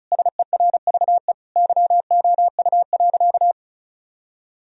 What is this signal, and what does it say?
Morse code, heard as a keyed tone, about 34 wpm, 700 Hz. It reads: SERVE YOU.
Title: Sentences from Top 1000 Words - Review Part 2 - 35wpm